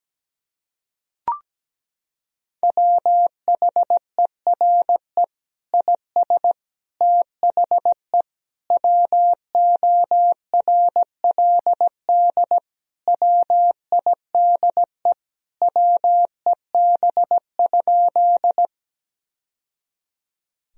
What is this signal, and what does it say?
Morse code, heard as a keyed tone, about 17 wpm, 700 Hz. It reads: WHERE IS THE WORLD WIDE WEB?
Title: Sentences from Top 1000 Words - Review Part 4 - 17wpm